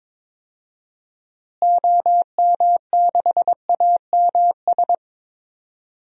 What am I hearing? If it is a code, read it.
Morse code, heard as a keyed tone, about 21 wpm, 700 Hz. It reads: OM6AMS
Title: DXCC-100 Call Signs - 6 Characters - Repeated 2x - 22wpm